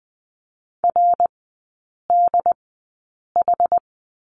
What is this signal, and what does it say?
Morse code, heard as a keyed tone, about 20 wpm, spaced as for 8 wpm, 700 Hz. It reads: RDH
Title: 067 Lesson 12e - Letter d - Three Characters 20wpm